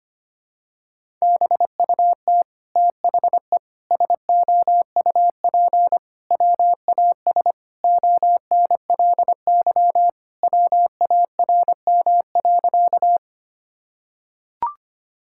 Morse code, heard as a keyed tone, about 25 wpm, 700 Hz. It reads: BUT THE SOUP WAS ONLY WARM.
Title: Sentences from Top 1000 Words - Review Part 6 - 25wpm